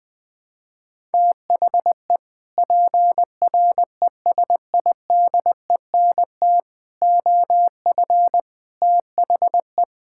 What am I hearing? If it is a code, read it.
Morse code, heard as a keyed tone, about 20 wpm, 700 Hz. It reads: THE PRESIDENT OF THE
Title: Sets of 6 Words - 20wpm